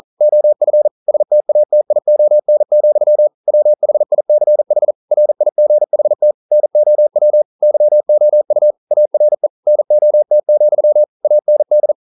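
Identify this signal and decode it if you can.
Morse code, heard as a keyed tone, about 39 wpm, 600 Hz. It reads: OF STATION, WHICH RIGHT NOW YOU ARE NOT, AND
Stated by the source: Morse code practice at 41 wpm, from CW med Gnister.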